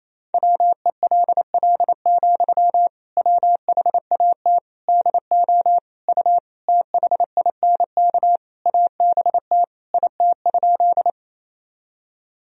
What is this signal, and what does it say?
Morse code, heard as a keyed tone, about 28 wpm, 700 Hz. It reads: WELL, WHAT DO U THINK ABT IT?
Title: Sentences from Top 200 Words and Abbreviations 28wpm